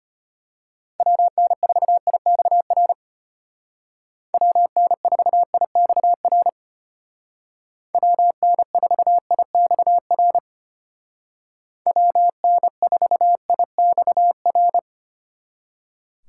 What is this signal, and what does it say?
Morse code, heard as a keyed tone, about 33 wpm, 700 Hz. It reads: WN4IXR WN4IXR WN4IXR WN4IXR